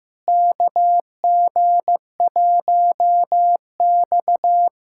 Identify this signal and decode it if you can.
Morse code, heard as a keyed tone, about 15 wpm, 700 Hz. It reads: KG1X
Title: USA Callsign DE Callsign - 4 Characters - 15wpm